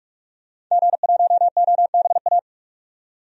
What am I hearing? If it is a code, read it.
Morse code, heard as a keyed tone, about 44 wpm, 700 Hz. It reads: G1OBA
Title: DXCC-100 Call Signs - 5 Characters - 45wpm